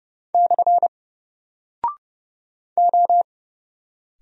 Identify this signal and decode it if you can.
Morse code, heard as a keyed tone, about 30 wpm, 700 Hz. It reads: / O
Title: Single Character (Letter:Number:Punctuation:Prosign) Alt 30wpm